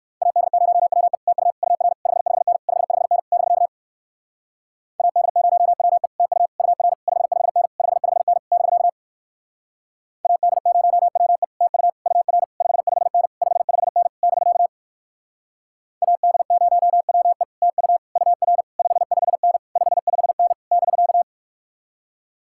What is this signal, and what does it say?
Morse code, heard as a keyed tone, about 56 wpm, 700 Hz. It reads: AD0WE TU UR 55N 55N <BK> AD0WE TU UR 55N 55N <BK> AD0WE TU UR 55N 55N <BK> AD0WE TU UR 55N 55N <BK>
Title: Speed-Racing Common QSO Phrases - 45wpm